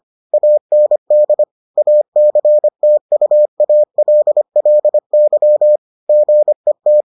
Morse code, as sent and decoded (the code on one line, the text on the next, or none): .- -. -.. / .- -.-. - ..- .- .-.. .-.. -.-- / --. -
AND ACTUALLY GET